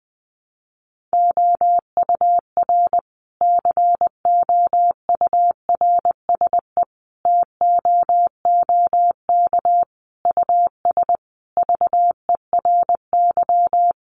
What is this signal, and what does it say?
Morse code, heard as a keyed tone, about 20 wpm, 700 Hz. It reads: OUR COURSE TOOK US VERY
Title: Sentences from Top 1300 Words 20wpm